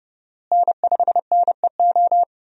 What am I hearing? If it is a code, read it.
Morse code, heard as a keyed tone, about 32 wpm, 700 Hz. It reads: N5NEO
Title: USA Call Signs - 5 Characters - Repeated 2x - 30wpm